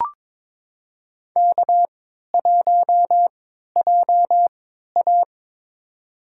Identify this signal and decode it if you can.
Morse code, heard as a keyed tone, about 22 wpm, spaced as for 13 wpm, 700 Hz. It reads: K1JA
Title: USA Call Signs - 3x Character Spacing - 22wpm